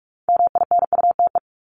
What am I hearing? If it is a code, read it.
Morse code, heard as a keyed tone, about 44 wpm, 700 Hz. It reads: MINUTE
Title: Top 500 Words Repeated 45wpm